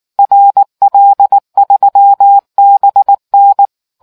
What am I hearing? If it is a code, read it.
Morse code, heard as a keyed tone, about 19 wpm, 800 Hz. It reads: RL3BN